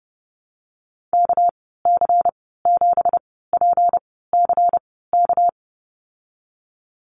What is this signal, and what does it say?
Morse code, heard as a keyed tone, about 30 wpm, spaced as for 17 wpm, 700 Hz. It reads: KC7PCK
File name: Lesson 37g - Letter x - Call Signs 30wpm